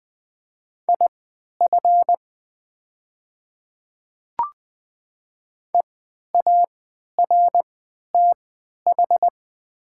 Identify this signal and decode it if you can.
Morse code, heard as a keyed tone, about 20 wpm, spaced as for 11 wpm, 700 Hz. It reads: IF EARTH
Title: 3x Character Spacing - Top 200 Words - 20wpm